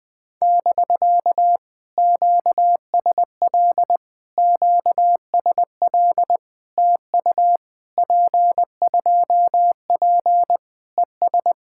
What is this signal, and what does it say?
Morse code, heard as a keyed tone, about 19 wpm, 700 Hz. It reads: <BK> QSL QSL TU P2P ES
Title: POTA QSO Practice - 20wpm